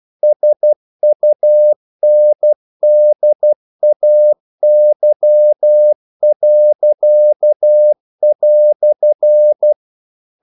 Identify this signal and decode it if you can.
Morse code, heard as a keyed tone, about 12 wpm, 600 Hz. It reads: SUNDAY."
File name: CW / Practice / EnglishQuotations0312